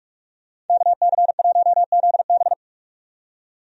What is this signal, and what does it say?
Morse code, heard as a keyed tone, about 43 wpm, 700 Hz. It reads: KC1ZB
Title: USA Call Signs - 5 Characters - 45wpm